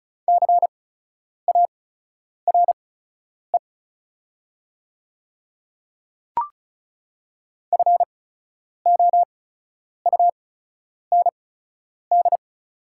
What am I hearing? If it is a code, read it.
Morse code, heard as a keyed tone, about 36 wpm, spaced as for 10 wpm, 700 Hz. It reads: CARE FOUND
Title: 8x Character Spacing - Top 300 Words - 35wpm